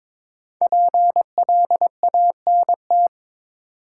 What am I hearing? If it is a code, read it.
Morse code, heard as a keyed tone, about 22 wpm, 700 Hz. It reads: PLANT
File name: 5 Letter Words - First 3 Letters Given - 22wpm